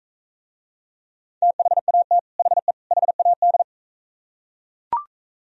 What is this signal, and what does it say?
Morse code, heard as a keyed tone, about 44 wpm, 700 Hz. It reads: THAT HE HAD